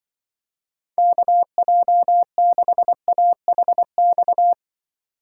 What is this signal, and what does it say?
Morse code, heard as a keyed tone, about 24 wpm, 700 Hz. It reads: KJ6AHX
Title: Speed Racing - USA 2x3 Call Signs 20wpm